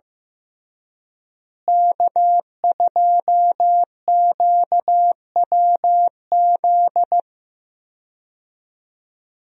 Morse code, heard as a keyed tone, about 15 wpm, 700 Hz. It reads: K2QWZ
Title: USA Call Signs - 5 Characters - Repeated 2x - 15wpm